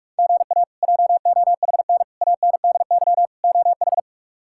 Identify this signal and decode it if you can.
Morse code, heard as a keyed tone, about 44 wpm, 700 Hz. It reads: GA JOHN ANDY OH
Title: K1USN SST QSO Practice - 45wpm